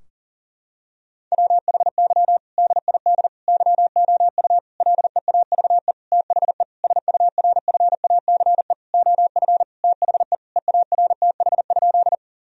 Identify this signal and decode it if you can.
Morse code, heard as a keyed tone, about 40 wpm, 700 Hz. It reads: WHY DID YOU LEAVE THE SURFACE OF THE EARTH?
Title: Sentences from Top 1000 Words - Review Part 3 - 40wpm